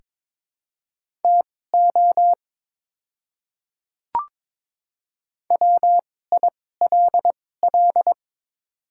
Morse code, heard as a keyed tone, about 22 wpm, spaced as for 16 wpm, 700 Hz. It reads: TO WILL
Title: 2x Character Spacing - Top 100 Words - 22wpm